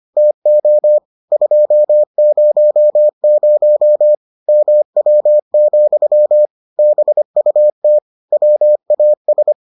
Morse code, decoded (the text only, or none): TO 200 MW, BUT WAS